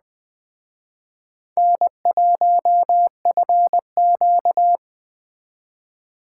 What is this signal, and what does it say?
Morse code, heard as a keyed tone, about 20 wpm, 700 Hz. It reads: N1FQ
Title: Speed Racing - USA 1x2 Call Signs 17wpm